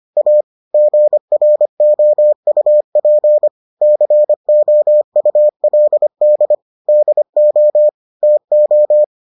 Morse code, heard as a keyed tone, about 25 wpm, 600 Hz. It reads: A GROUP COULD DO TO